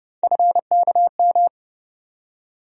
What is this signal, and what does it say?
Morse code, heard as a keyed tone, about 30 wpm, 700 Hz. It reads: FKM
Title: USA Call Sign Postfixes - 3 Letters 30wpm